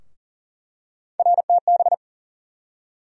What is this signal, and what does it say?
Morse code, heard as a keyed tone, about 40 wpm, 700 Hz. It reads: RTB
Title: USA Call Sign Postfixes - 3 Letters 40wpm